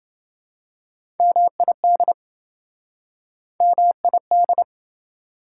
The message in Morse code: MID MID

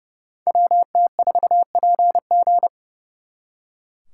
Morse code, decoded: WT4PG

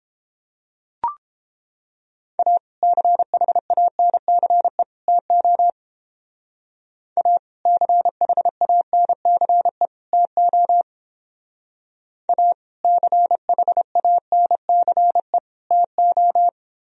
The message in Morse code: A CHANCE TO A CHANCE TO A CHANCE TO

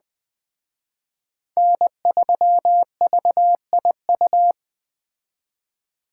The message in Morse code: N3VIU

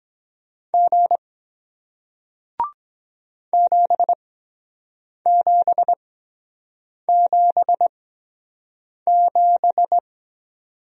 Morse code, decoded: G 7777

